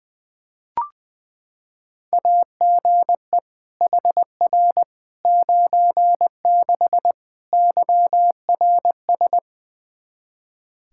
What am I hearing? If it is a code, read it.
Morse code, heard as a keyed tone, about 20 wpm, 700 Hz. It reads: AGE HR 96 YRS
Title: K1USN SST QSO Practice 20wpm